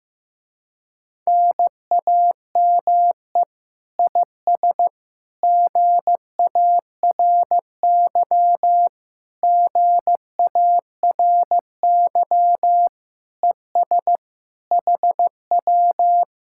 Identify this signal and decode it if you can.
Morse code, heard as a keyed tone, about 15 wpm, 700 Hz. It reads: NAME IS GARY GARY ES HW
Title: Speed-Racing Common QSO Phrases Encore - 15wpm